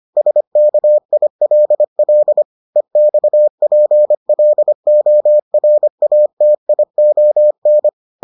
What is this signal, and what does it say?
Morse code, heard as a keyed tone, about 25 wpm, 600 Hz. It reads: SKILL EXPLORATION